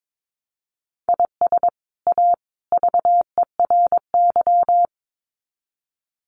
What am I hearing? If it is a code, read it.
Morse code, heard as a keyed tone, about 22 wpm, 700 Hz. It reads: IS A VERY
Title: Sets of 3 Words 22wpm